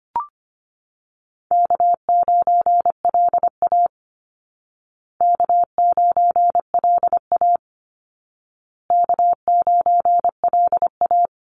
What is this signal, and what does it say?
Morse code, heard as a keyed tone, about 25 wpm, 700 Hz. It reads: K9LA K9LA K9LA
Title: Easy Call Signs Repeated 3x - 25wpm